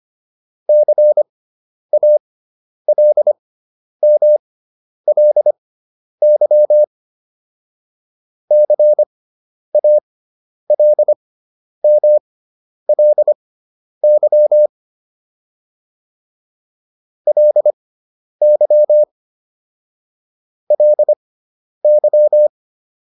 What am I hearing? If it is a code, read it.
Morse code, heard as a keyed tone, about 25 wpm, spaced as for 10 wpm, 600 Hz. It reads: CALMLY CALMLY LY LY